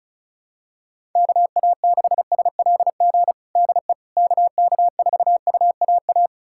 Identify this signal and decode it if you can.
Morse code, heard as a keyed tone, about 36 wpm, 700 Hz. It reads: KA6SLG DE KK4UAA
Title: USA Callsign DE Callsign - 6 Characters - 35wpm